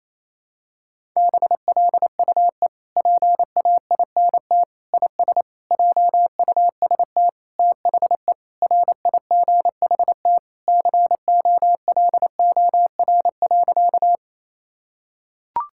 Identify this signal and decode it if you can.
Morse code, heard as a keyed tone, about 27 wpm, 700 Hz. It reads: BLUE PAINT IS JUST THE RIGHT COLOR.
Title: Sentences from Top 1000 Words - Review Part 6 - 28wpm